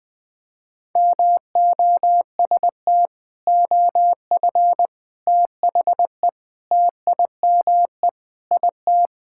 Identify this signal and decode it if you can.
Morse code, heard as a keyed tone, about 20 wpm, 700 Hz. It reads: MOST OF THE TIME IT
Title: Sentences with 4-Word Phrases - 20wpm